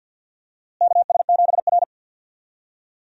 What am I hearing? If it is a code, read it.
Morse code, heard as a keyed tone, about 48 wpm, 700 Hz. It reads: KS7R
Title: USA Call Signs - 4 Characters - Repeated 3x - 50wpm